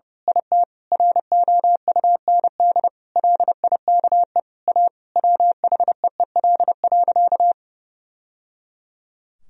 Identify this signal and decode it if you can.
Morse code, heard as a keyed tone, about 30 wpm, 700 Hz. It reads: IT ROUND LIKE A WHEEL.
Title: Sentences from Top 1000 Words - Review Part 3 - 30wpm